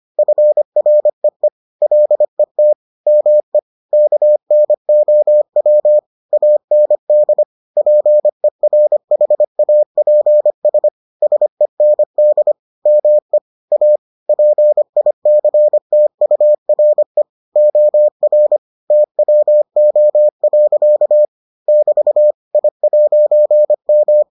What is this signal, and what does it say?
Morse code, heard as a keyed tone, about 25 wpm, 600 Hz. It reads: FREE LET ME KNOW AND PERHAPS SEND ME A PICTURE OR TWO. <BT> I'M